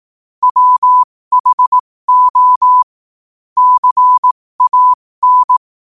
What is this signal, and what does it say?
Morse code, heard as a keyed tone, about 18 wpm, spaced as for 15 wpm, 1000 Hz. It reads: WHO CAN
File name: qotdfiles / 15wpm